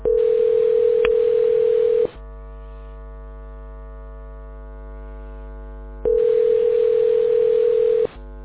Ringback tone (North America), 440 + 480 Hz.